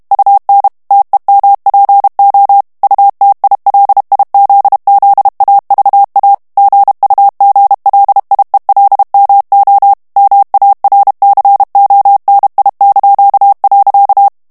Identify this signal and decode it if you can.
Morse code, heard as a keyed tone, about 32 wpm, 800 Hz. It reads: UN TEMPO UTILIZZAVA GUGLIELMO MARCONI).
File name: ITALIEN texte clair 1 vitesse 30 mots minute